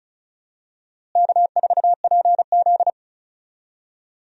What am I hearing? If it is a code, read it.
Morse code, heard as a keyed tone, about 36 wpm, 700 Hz. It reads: K4PZ
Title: USA Call Signs - 4 Characters - Repeated 3x - 35wpm